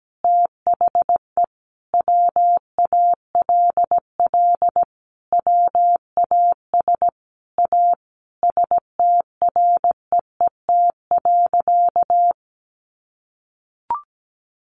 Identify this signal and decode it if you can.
Morse code, heard as a keyed tone, about 17 wpm, 700 Hz. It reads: THE WALL WAS A STREET.
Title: Sentences from Top 1100 Words 17wpm